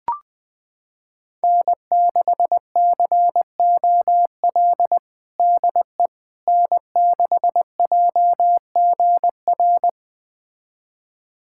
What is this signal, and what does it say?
Morse code, heard as a keyed tone, about 20 wpm, 700 Hz. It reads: N6COL DE N6JGR
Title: USA Callsign DE Callsign - 5 Characters - 20wpm